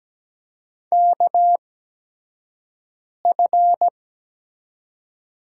Morse code, heard as a keyed tone, about 17 wpm, spaced as for 5 wpm, 700 Hz. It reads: KF
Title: USA Call Signs - 6 Characters - 8x Character Spacing - 17wpm